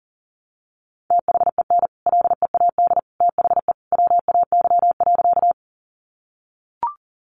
Morse code, heard as a keyed tone, about 40 wpm, 700 Hz. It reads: THEN LEAD THE WAY.
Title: Sentences from Top 500 Words 40wpm